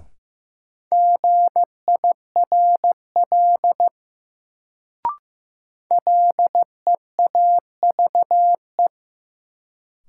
Morse code, heard as a keyed tone, about 15 wpm, 700 Hz. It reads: GIRL LEAVE
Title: Sentences from Top 400 Words - New Word Drill - 15wpm